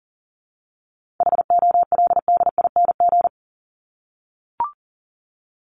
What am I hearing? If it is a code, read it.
Morse code, heard as a keyed tone, about 40 wpm, 700 Hz. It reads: HOLDING
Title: ING Suffix - Top 100 Words with Base Words 40wpm